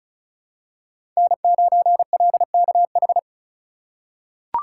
Morse code, heard as a keyed tone, about 36 wpm, 700 Hz. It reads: N9LKH
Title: USA Call Signs - 5 Characters - 3x Character Spacing - 35wpm